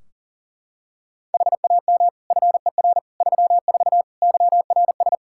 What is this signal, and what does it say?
Morse code, heard as a keyed tone, about 40 wpm, 700 Hz. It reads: HAM FER 34 YRS
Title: Practice QSO - Exchange 3 - More Info - 40wpm